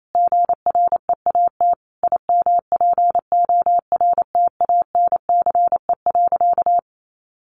Morse code, heard as a keyed tone, about 28 wpm, 700 Hz. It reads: GREAT IMPORTANCE.
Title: Sentences from Top 1200 Words 28wpm